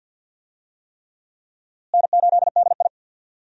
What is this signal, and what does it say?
Morse code, heard as a keyed tone, about 49 wpm, 700 Hz. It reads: N8DI
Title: USA Call Signs - 4 Characters - Repeated 3x - 50wpm